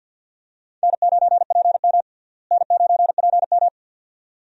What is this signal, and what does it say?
Morse code, heard as a keyed tone, about 47 wpm, 700 Hz. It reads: N9PM N9PM
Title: USA Call Signs - 4 Characters - Repeated 2x - 50wpm